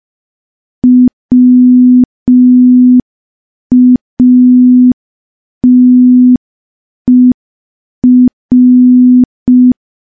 Morse code, heard as a keyed tone, about 5 wpm, 250 Hz. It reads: WATER
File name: morse / morse-training-05WPM